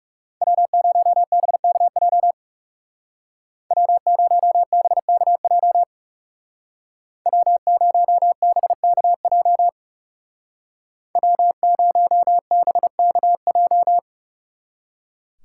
Morse code, heard as a keyed tone, about 39 wpm, 700 Hz. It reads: W0BKJ W0BKJ W0BKJ W0BKJ